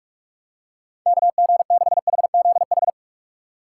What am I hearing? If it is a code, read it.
Morse code, heard as a keyed tone, about 42 wpm, 700 Hz. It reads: KG6HZH
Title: USA Call Signs - 3x Character Spacing - 45wpm